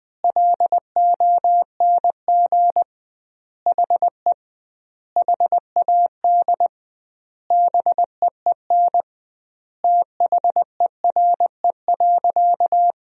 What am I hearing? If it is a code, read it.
Morse code, heard as a keyed tone, about 20 wpm, 700 Hz. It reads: LONG HE HAD BEEN THERE.
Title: 2x Word-Spacing - Sentences from Top 300 Words - 20wpm